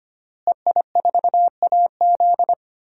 Morse code, heard as a keyed tone, about 25 wpm, 700 Hz. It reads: EI4AZ
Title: DXCC-100 Call Signs - 5 Characters - Repeated 2x - 25wpm